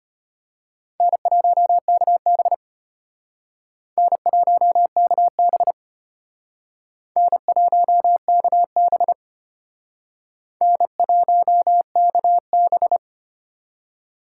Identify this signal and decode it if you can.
Morse code, heard as a keyed tone, about 32 wpm, 700 Hz. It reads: N1KB N1KB N1KB N1KB